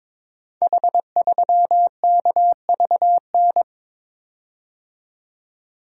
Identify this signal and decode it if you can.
Morse code, heard as a keyed tone, about 22 wpm, 700 Hz. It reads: H3KVN
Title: Code Groups - 5 Letter:Numbers - 22wpm